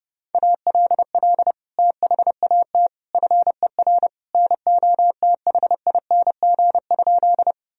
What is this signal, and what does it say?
Morse code, heard as a keyed tone, about 30 wpm, 700 Hz. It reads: ALL THAT FER NOTHING?
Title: Sentences from Top 500 Words and Abbreviations 30wpm